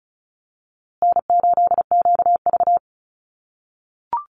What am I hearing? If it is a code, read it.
Morse code, heard as a keyed tone, about 34 wpm, 700 Hz. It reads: N8QV
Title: USA Call Signs Repeated 2x 35wpm